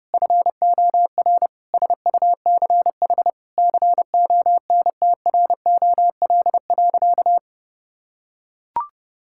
Morse code, heard as a keyed tone, about 32 wpm, 700 Hz. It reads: FOR SUCH CONTROL.
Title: Sentences from Top 1000 Words - Review Part 5 - 30wpm